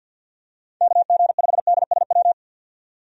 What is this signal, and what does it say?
Morse code, heard as a keyed tone, about 47 wpm, 700 Hz. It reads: KG5DSW
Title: USA Call Signs - 6 Characters - 50wpm